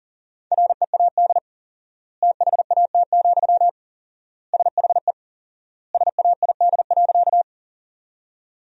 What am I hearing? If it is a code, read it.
Morse code, heard as a keyed tone, about 40 wpm, 700 Hz. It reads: READ THAT, SHE SAID.
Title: 4x Word-Spacing - Sentences from Top 200 Words - 40wpm